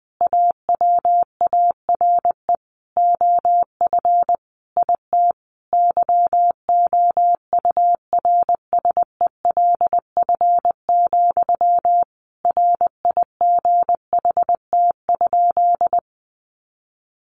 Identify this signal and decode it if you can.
Morse code, heard as a keyed tone, about 20 wpm, 700 Hz. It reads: AWARE OF IT YOURSELF, RIGHT?
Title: Sentences from Top 1200 Words 20wpm